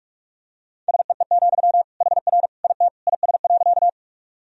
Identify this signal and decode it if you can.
Morse code, heard as a keyed tone, about 46 wpm, 700 Hz. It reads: SEE, HR IT IS.
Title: Sentences from Top 200 Words and Abbreviations 45wpm